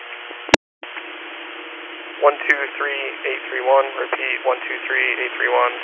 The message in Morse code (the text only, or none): none